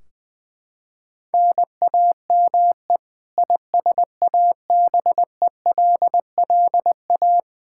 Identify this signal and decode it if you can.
Morse code, heard as a keyed tone, about 20 wpm, 700 Hz. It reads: NAME ISABELLA
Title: K1USN SST QSO Practice 20wpm